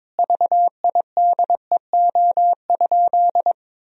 Morse code, heard as a keyed tone, about 22 wpm, 700 Hz. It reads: VIDEO?
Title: Sentences from Top 1000 Words - Review Part 4 - 22wpm